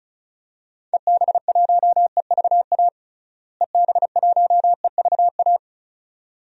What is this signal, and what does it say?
Morse code, heard as a keyed tone, about 36 wpm, 700 Hz. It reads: EB1EVA EB1EVA
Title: DXCC-100 Call Signs - 6 Characters - Repeated 2x - 35wpm